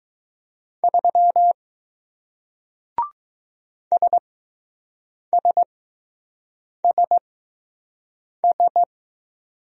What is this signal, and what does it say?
Morse code, heard as a keyed tone, about 20 wpm, spaced as for 6 wpm, 700 Hz. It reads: 3 SSSS